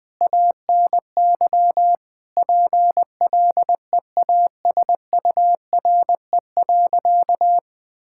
Morse code, heard as a keyed tone, about 20 wpm, 700 Hz. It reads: ANY PLEASURE.
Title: Sentences from Top 1600 Words 20wpm